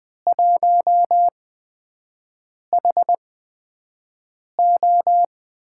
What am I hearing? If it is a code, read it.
Morse code, heard as a keyed tone, about 21 wpm, spaced as for 6 wpm, 700 Hz. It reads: 1HO
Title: USA Call Signs - 5 Characters - 8x Character Spacing - 20wpm